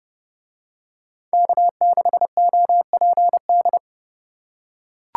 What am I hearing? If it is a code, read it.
Morse code, heard as a keyed tone, about 29 wpm, 700 Hz. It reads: K6OPD